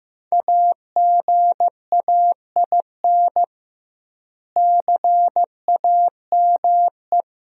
Morse code, heard as a keyed tone, about 16 wpm, 700 Hz. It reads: AGAIN CAME
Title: 2x Word-Spacing - Sentences from Top 500 Words - 15wpm